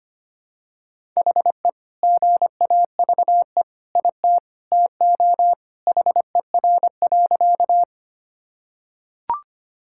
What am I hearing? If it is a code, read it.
Morse code, heard as a keyed tone, about 25 wpm, 700 Hz. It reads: HE GAVE IT TO HER.
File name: practice / 25wpm / Sentences with 2-Word Phrases - 25wpm